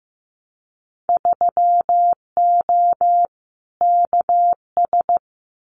Morse code, heard as a keyed tone, about 15 wpm, 700 Hz. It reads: 3O KS
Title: Winter Field Day Exchanges 15wpm